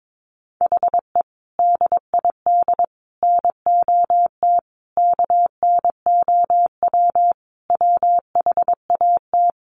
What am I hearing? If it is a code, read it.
Morse code, heard as a keyed tone, about 22 wpm, 700 Hz. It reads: HE DID NOT KNOW WHAT
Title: Sentences from Top 600 Words 22wpm